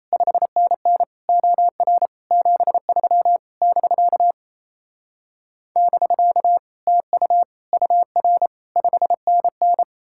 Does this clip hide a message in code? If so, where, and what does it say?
Yes: Morse code from 0.1 to 9.8 s, heard as a keyed tone, about 37 wpm, 700 Hz. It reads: 5NN OR 73 <BK> <BK> TU UR 5NN